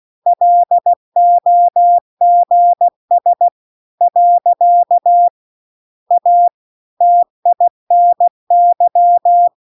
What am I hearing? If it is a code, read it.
Morse code, heard as a keyed tone, about 16 wpm, 700 Hz. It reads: LOGS . A TINY